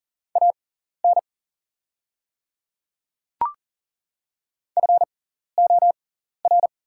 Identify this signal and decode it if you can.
Morse code, heard as a keyed tone, about 41 wpm, spaced as for 14 wpm, 690 Hz. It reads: AN FOR
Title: 6x Character Spacing - Top 100 Words - 40wpm